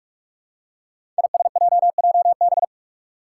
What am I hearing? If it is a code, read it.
Morse code, heard as a keyed tone, about 44 wpm, 700 Hz. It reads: ISJJB
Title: Five Challenging Letter-Number Combinations 45wpm